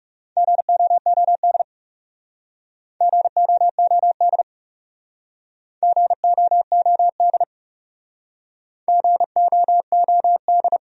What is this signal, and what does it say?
Morse code, heard as a keyed tone, about 40 wpm, 700 Hz. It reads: GOOD GOOD GOOD GOOD